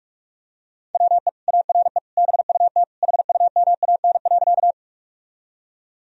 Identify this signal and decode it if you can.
Morse code, heard as a keyed tone, about 46 wpm, 700 Hz. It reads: WE ARE BUT HUMAN.
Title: Sentences from Top 1000 Words - Review Part 3 - 45wpm